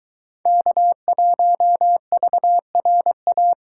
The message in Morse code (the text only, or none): K1VRA